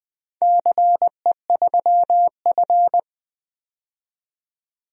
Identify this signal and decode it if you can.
Morse code, heard as a keyed tone, about 20 wpm, 700 Hz. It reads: CE3F
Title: DXCC-100 Call Signs - 4 Characters - 20wpm